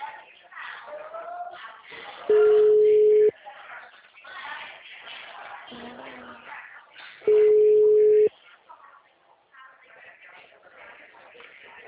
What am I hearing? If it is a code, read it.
Ringback tone (Europe, 425 Hz), with speech.
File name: out- -2018- - - .9703